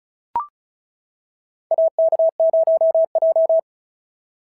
Morse code, heard as a keyed tone, about 34 wpm, 690 Hz. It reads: AK0J
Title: SOTA QSO Practice 35wpm